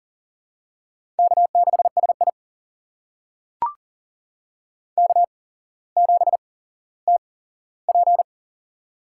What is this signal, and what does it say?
Morse code, heard as a keyed tone, about 40 wpm, spaced as for 17 wpm, 700 Hz. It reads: K6SI K7TP
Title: USA Call Signs - 4 Characters - 8x Character Spacing - 40wpm